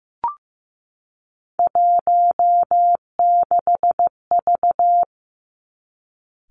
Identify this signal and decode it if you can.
Morse code, heard as a keyed tone, about 15 wpm, 700 Hz. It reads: 16V
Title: Three Challenging Letter-Number Combinations 15wpm